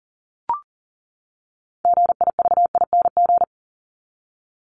Morse code, heard as a keyed tone, about 40 wpm, 700 Hz. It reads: GIVING
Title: ING Suffix - Top 100 Words with Base Words 40wpm